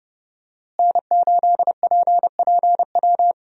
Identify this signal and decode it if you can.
Morse code, heard as a keyed tone, about 30 wpm, 700 Hz. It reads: N8PPW
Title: USA Call Signs - 5 Characters - Repeated 3x - 30wpm